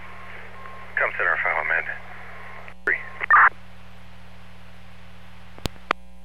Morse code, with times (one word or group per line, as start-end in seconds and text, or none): none